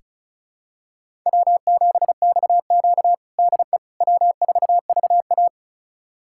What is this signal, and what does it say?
Morse code, heard as a keyed tone, about 36 wpm, 700 Hz. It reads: W7XQ DE W4VA
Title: USA Callsign DE Callsign - 4 Characters - 35wpm